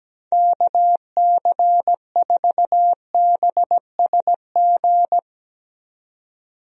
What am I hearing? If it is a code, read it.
Morse code, heard as a keyed tone, about 17 wpm, 700 Hz. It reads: KC4BSG
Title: USA Call Signs - 6 Characters - 17wpm